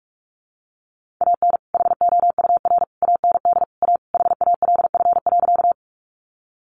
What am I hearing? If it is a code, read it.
Morse code, heard as a keyed tone, about 44 wpm, 700 Hz. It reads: AN HOUR AND A HALF.
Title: Sentences from Top 400 Words 45wpm